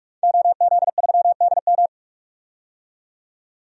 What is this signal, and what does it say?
Morse code, heard as a keyed tone, about 46 wpm, 700 Hz. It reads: OZ3DM
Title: DXCC-100 Call Signs - 5 Characters - 45wpm